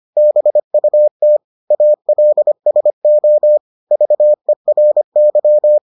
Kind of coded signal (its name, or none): Morse code